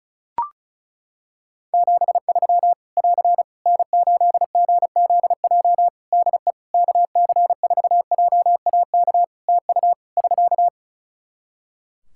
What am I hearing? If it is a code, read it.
Morse code, heard as a keyed tone, about 36 wpm, 700 Hz. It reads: 73 <AR> N8GZJ DE KC4JAK TU <SK>
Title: Practice QSO - Exchange 4 - Ending - 35wpm